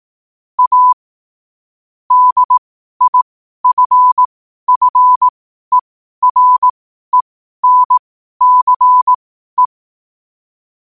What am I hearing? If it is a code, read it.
Morse code, heard as a keyed tone, about 18 wpm, spaced as for 12 wpm, 1000 Hz. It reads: A DIFFERENCE
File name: qotdfiles / 12wpm